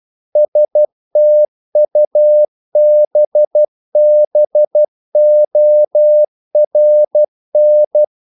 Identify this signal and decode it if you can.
Morse code, heard as a keyed tone, about 12 wpm, 600 Hz. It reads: STUBBORN
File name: mp3 files / the gardener - 12wpm